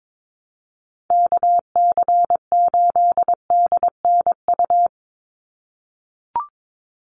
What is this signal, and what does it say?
Morse code, heard as a keyed tone, about 22 wpm, 700 Hz. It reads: KC8DNU
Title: USA Call Signs 22wpm